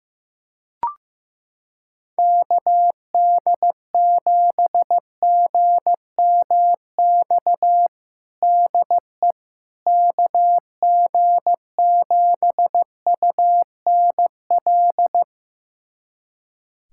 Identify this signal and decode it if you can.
Morse code, heard as a keyed tone, about 15 wpm, 700 Hz. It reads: KD7GMX DE KG7UNL
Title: USA Callsign DE Callsign - 6 Characters - 15wpm